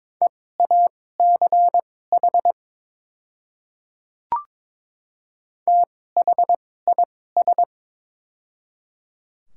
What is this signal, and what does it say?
Morse code, heard as a keyed tone, about 22 wpm, spaced as for 16 wpm, 690 Hz. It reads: EACH THIS
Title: 2x Character Spacing - Top 100 Words - 22wpm